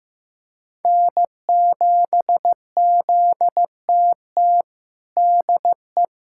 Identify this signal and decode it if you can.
Morse code, heard as a keyed tone, about 15 wpm, 700 Hz. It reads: N7ZTT DE
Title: USA Callsign DE Callsign - 5 Characters - 15wpm